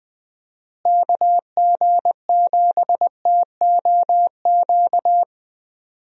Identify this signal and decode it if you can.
Morse code, heard as a keyed tone, about 20 wpm, 700 Hz. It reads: KG7TOQ
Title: USA Call Signs - 6 Characters - Repeated 3x - 20wpm